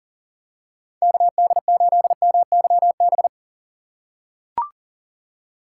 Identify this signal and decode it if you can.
Morse code, heard as a keyed tone, about 40 wpm, 700 Hz. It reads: KD8MYB
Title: USA Call Signs - 6 Characters - 2x Character Spacing - 40wpm